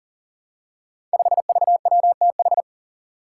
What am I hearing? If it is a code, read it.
Morse code, heard as a keyed tone, about 40 wpm, 700 Hz. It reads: 5VWTH
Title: Five Challenging Letter-Number Combinations 40wpm